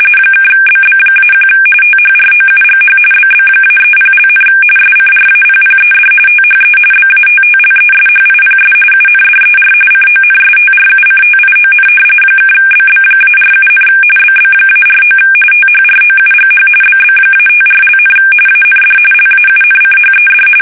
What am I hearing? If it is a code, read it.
RTTY, about 45.45 baud. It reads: (BB-1 THROUGH BB-30), SCRAPPED THE BATTLESHIPS UNDER CONSTRUCTION (BB-47, BB-49 THROUGH BB-54), AND LAUNCHED
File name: fsk850 / battleshipnj